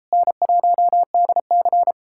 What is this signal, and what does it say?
Morse code, heard as a keyed tone, about 38 wpm, 700 Hz. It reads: N1DC